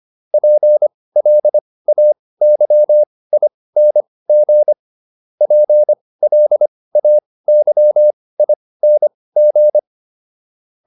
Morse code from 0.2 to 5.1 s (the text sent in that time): PLAYING